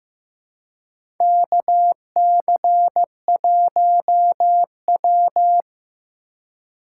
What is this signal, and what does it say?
Morse code, heard as a keyed tone, about 15 wpm, 700 Hz. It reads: KC1W
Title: USA Call Signs - 4 Characters - 4x Character Spacing - 15wpm